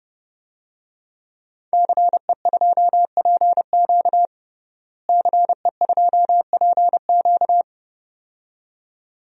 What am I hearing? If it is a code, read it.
Morse code, heard as a keyed tone, about 30 wpm, 700 Hz. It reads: CE2PQ CE2PQ
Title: DXCC-100 Call Signs - 5 Characters - Repeated 2x - 30wpm